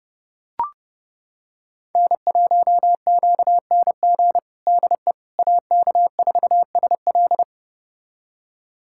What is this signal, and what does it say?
Morse code, heard as a keyed tone, about 32 wpm, 700 Hz. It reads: N1QNG DE AK4SL
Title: USA Callsign DE Callsign - 5 Characters - 30wpm